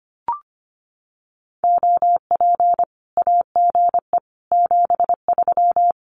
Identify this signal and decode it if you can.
Morse code, heard as a keyed tone, about 25 wpm, 700 Hz. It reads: OP AGE 73
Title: Common QSO Phrases 25wpm